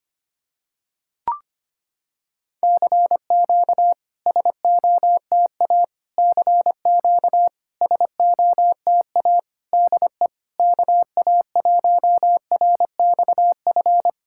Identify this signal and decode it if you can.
Morse code, heard as a keyed tone, about 25 wpm, 700 Hz. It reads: CQ SOTA CQ SOTA DE KA1RXF
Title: SOTA QSO Practice 25wpm